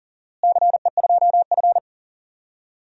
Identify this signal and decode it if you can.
Morse code, heard as a keyed tone, about 40 wpm, 700 Hz. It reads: CE2F
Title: DXCC-100 Call Signs - 4 Characters - Repeated 3x - 40wpm